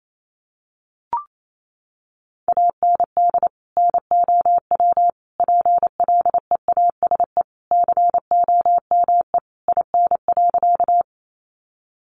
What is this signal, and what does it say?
Morse code, heard as a keyed tone, about 28 wpm, 700 Hz. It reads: AND NOW PLEASE COME IN.
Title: Sentences from Top 1500 Words 28wpm